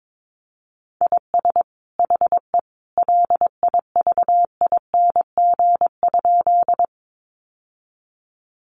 Morse code, heard as a keyed tone, about 22 wpm, 700 Hz. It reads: IS HE LIVING?
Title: Sentences from Top 1100 Words 22wpm